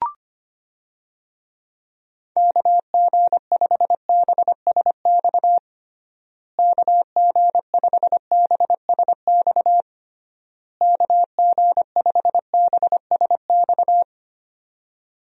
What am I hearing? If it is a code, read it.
Morse code, heard as a keyed tone, about 25 wpm, 700 Hz. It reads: KG5BSX KG5BSX KG5BSX
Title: USA Call Signs - 6 Characters - Repeated 3x - 25wpm